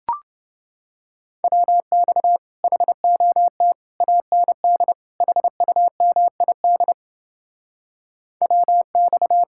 Morse code, heard as a keyed tone, about 30 wpm, 700 Hz. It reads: WX HOT AND HUMID WX